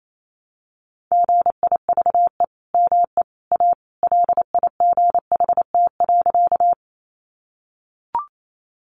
Morse code, heard as a keyed tone, about 28 wpm, 700 Hz. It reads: GIVE ME A LIGHT.
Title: Sentences from Top 200 Words 28wpm